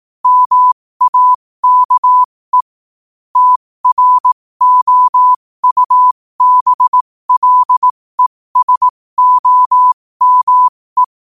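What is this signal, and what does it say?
Morse code, heard as a keyed tone, about 18 wpm, spaced as for 15 wpm, 1000 Hz. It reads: MAKE TROUBLESOME
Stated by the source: Morse code quote of the day sent at 15 words per minute.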